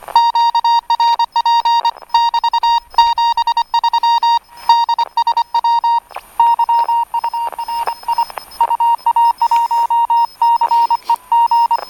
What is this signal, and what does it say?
Morse code, heard as a keyed tone, about 25 wpm, 930 Hz. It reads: QRP <BT> 73 DSW YURI UA1CEG
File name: attachments / 19.04.2017 OK1FIM RX DSB